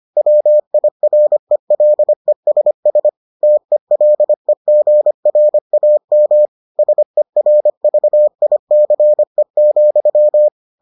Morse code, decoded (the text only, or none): WIRELESS TELEGRAM SERVICE,